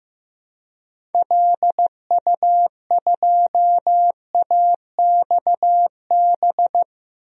Morse code, heard as a keyed tone, about 15 wpm, 700 Hz. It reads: LU2AXB
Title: DXCC-100 Call Signs - 6 Characters - Repeated 3x - 15wpm